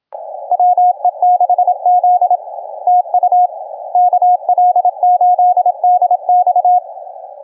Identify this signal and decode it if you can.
Morse code, heard as a keyed tone, about 27 wpm, 750 Hz, with radio noise. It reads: WE6Z TU KL8DX
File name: cqww cw 10 minutes